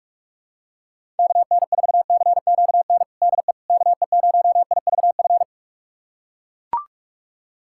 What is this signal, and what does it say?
Morse code, heard as a keyed tone, about 41 wpm, 700 Hz. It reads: KN4CQN DE KE0IVF
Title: USA Callsign DE Callsign - 6 Characters - 45wpm